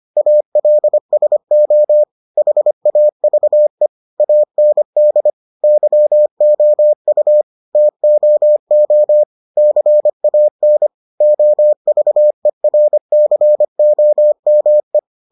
Morse code, heard as a keyed tone, about 25 wpm, 600 Hz. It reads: ALSO HAVE AND YOU TOO CAN OVERCOME